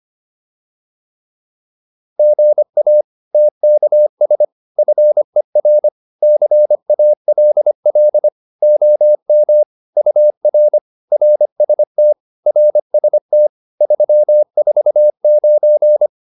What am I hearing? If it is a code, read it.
Morse code, heard as a keyed tone, about 25 wpm, 600 Hz. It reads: GA TKS FER CALL OM UR RST RST 349
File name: QSO408a 25